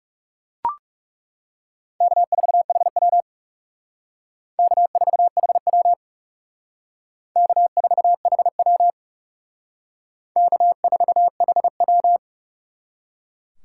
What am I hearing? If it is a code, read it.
Morse code, heard as a keyed tone, about 37 wpm, 700 Hz. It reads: K4HW K4HW K4HW K4HW
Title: Speed Racing - USA 1x2 Call Signs 30wpm